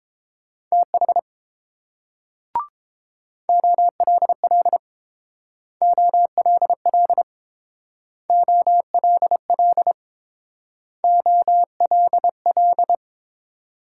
Morse code, heard as a keyed tone, about 30 wpm, 700 Hz. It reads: TH OLL OLL OLL OLL